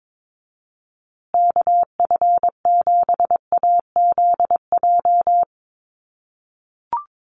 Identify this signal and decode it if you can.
Morse code, heard as a keyed tone, about 22 wpm, 700 Hz. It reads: KF7AZJ
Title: USA Call Signs Repeated 3x 22wpm